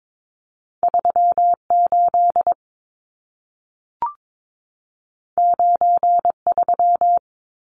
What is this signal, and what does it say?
Morse code, heard as a keyed tone, about 22 wpm, 700 Hz. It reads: 38 93
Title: Numbers - 2-Digit 22wpm